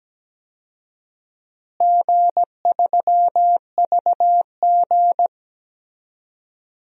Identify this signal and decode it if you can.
Morse code, heard as a keyed tone, about 17 wpm, 700 Hz. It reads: G3VG
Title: DXCC-100 Call Signs - 4 Characters - Repeated 3x - 17wpm